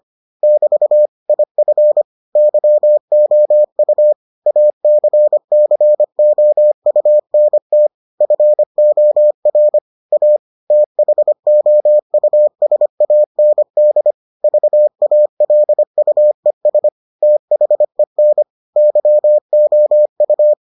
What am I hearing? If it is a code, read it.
Morse code, heard as a keyed tone, about 25 wpm, 600 Hz. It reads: <BT> IF YOU ACCOUNT FOR A THOUSAND VALUES THEN YOU